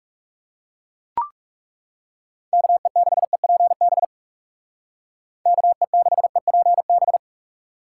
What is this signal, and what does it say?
Morse code, heard as a keyed tone, about 37 wpm, 700 Hz. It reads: KE6EPB KE6EPB
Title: Speed Racing - USA 2x3 Call Signs 30wpm